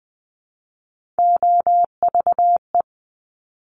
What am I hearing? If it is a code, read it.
Morse code, heard as a keyed tone, about 20 wpm, 700 Hz. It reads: OVE
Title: Top Three Letter Combinations in Words 20wpm